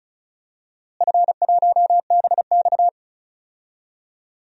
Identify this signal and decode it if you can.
Morse code, heard as a keyed tone, about 34 wpm, 700 Hz. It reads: F1BX
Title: DXCC-100 Call Signs - 4 Characters - Repeated 2x - 35wpm